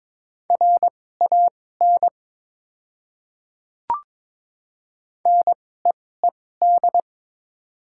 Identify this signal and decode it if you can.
Morse code, heard as a keyed tone, about 22 wpm, spaced as for 16 wpm, 700 Hz. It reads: RAN NEED
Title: 2x Character Spacing - Top 500 Words - 22wpm